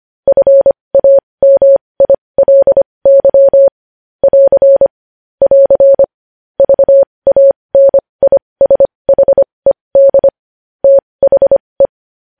Morse code, heard as a keyed tone, about 25 wpm, spaced as for 20 wpm, 550 Hz. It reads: FAMILY <AR> <AR> VANISHED THE
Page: News Headlines in Morse Code at 20 WPM